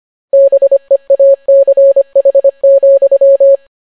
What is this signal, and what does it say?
Morse code, heard as a keyed tone, about 25 wpm, 550 Hz. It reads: BEACH,